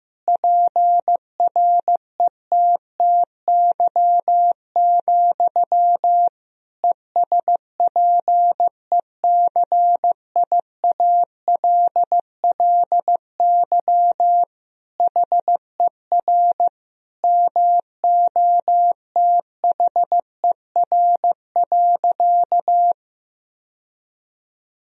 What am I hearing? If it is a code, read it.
Morse code, heard as a keyed tone, about 15 wpm, 700 Hz. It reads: PRETTY, ESPECIALLY HER MOTHER.
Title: Sentences from Top 1000 Words - Review Part 4 - 15wpm